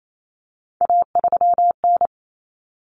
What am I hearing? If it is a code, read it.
Morse code, heard as a keyed tone, about 28 wpm, 700 Hz. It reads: A3N
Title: Common Cut Numbers - 3-Digit 28wpm